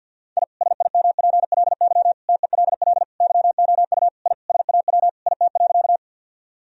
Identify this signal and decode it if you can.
Morse code, heard as a keyed tone, about 50 wpm, 700 Hz. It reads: I SIMPLY TELL YOU I SAW IT.
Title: Sentences from Top 1000 Words - Review Part 4 - 50wpm